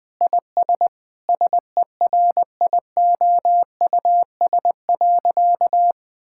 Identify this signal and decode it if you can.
Morse code, heard as a keyed tone, about 21 wpm, 700 Hz. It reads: IS SERIOUS.
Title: Sentences with 3-Word Phrases - 20wpm